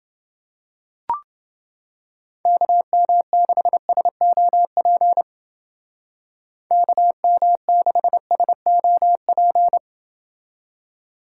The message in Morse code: KM6SOP KM6SOP

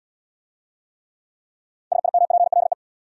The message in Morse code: SIMPLE

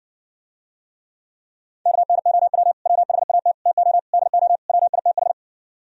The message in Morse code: KNOW WHAT TO DO WITH

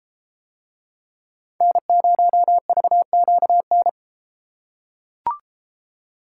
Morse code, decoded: N0VQN

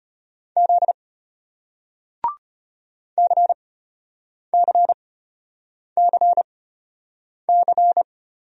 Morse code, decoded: Z CCCC

